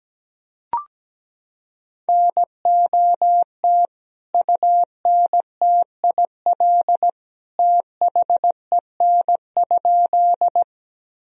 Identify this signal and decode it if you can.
Morse code, heard as a keyed tone, about 17 wpm, 700 Hz. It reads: NOT UNTIL THEN?
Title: Sentences from Top 1000 Words - Review Part 1 - 17wpm